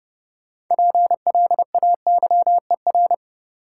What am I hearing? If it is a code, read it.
Morse code, heard as a keyed tone, about 30 wpm, 700 Hz. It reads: PLAYER
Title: Sentences from Top 1000 Words - Review Part 4 - New Word Drill - 30wpm